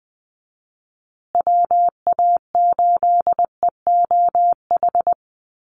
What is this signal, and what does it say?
Morse code, heard as a keyed tone, about 20 wpm, 700 Hz. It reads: WA8EOH
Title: USA Call Signs Repeated 2x 20wpm